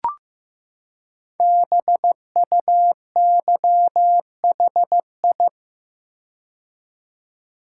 Morse code, heard as a keyed tone, about 15 wpm, 700 Hz. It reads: BUYHI
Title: Code Groups - 5 Letters - 15wpm